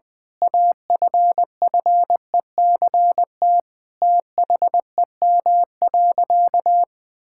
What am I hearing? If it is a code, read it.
Morse code, heard as a keyed tone, about 20 wpm, 700 Hz. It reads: AFFECT THEM.
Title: Sentences from Top 1000 Words - Review Part 6 - 20wpm